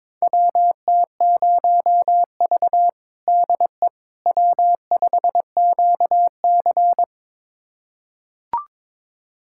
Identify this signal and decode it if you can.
Morse code, heard as a keyed tone, about 22 wpm, 700 Hz. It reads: WT0V DE W5QC
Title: USA Callsign DE Callsign - 4 Characters - 22wpm